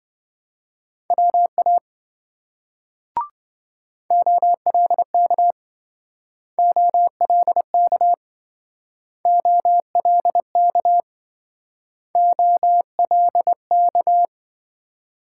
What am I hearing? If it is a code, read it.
Morse code, heard as a keyed tone, about 27 wpm, 700 Hz. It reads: WA OLK OLK OLK OLK